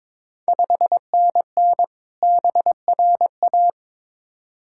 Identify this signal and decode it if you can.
Morse code, heard as a keyed tone, about 22 wpm, 700 Hz. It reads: 5NN BRA
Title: Ontario QSO Party Exchange 22wpm